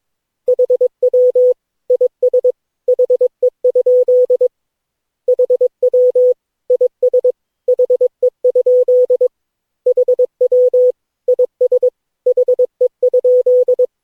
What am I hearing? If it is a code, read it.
Morse code, heard as a keyed tone, about 22 wpm, 500 Hz. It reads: HW IS HE? HW IS HE? HW IS HE?
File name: cw / Session9 cw training